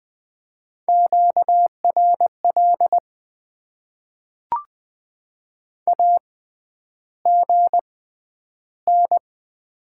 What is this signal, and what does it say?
Morse code, heard as a keyed tone, about 20 wpm, spaced as for 10 wpm, 700 Hz. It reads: QRL AGN
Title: Top 90 QSO Elements - 6x Character Spacing - 20wpm